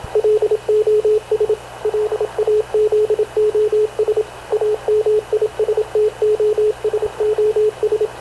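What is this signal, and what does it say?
Morse code, heard as a keyed tone, about 27 wpm, 430 Hz, with radio noise. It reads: LOS LAZOS AMISTOSOS
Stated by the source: Havana Radio / CLA sending Spanish press (PX) in Morse, recorded by Radio Officer Paul Dumesnil onboard CNAV QUEST/CZDO.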